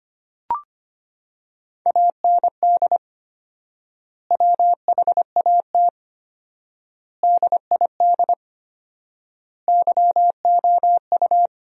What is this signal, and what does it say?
Morse code, heard as a keyed tone, about 25 wpm, 700 Hz. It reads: AND WHAT DID YOU
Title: 4x Word-Spacing - Sentences from Top 300 Words - 25wpm